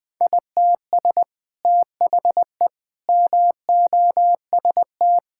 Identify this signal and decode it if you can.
Morse code, heard as a keyed tone, about 20 wpm, 700 Hz. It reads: ITS THE MOST
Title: Sets of 8 Words - 20wpm